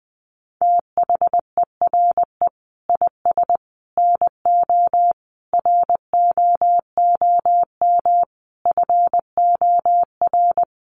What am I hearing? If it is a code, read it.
Morse code, heard as a keyed tone, about 20 wpm, 700 Hz. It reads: THERE IS NO ROOM FOR
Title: Sentences from Top 1200 Words 20wpm